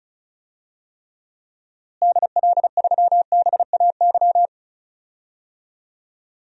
Morse code, heard as a keyed tone, about 36 wpm, 700 Hz. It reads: DL3BAY
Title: DXCC-100 Call Signs - 6 Characters - Repeated 3x - 35wpm